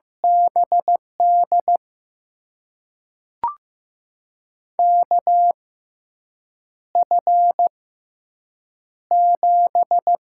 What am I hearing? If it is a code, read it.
Morse code, heard as a keyed tone, about 15 wpm, spaced as for 7 wpm, 700 Hz. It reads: BD KF7
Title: USA Call Signs - 6 Characters - 6x Character Spacing - 15wpm